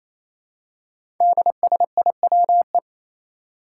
Morse code, heard as a keyed tone, about 27 wpm, 700 Hz. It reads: DSIWE
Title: Five Challenging Letter-Number Combinations 28wpm